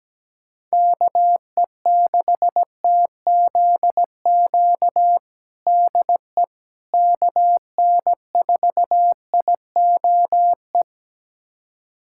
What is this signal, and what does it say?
Morse code, heard as a keyed tone, about 17 wpm, 700 Hz. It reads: KE6TZQ DE KN4IOE
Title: USA Callsign DE Callsign 17wpm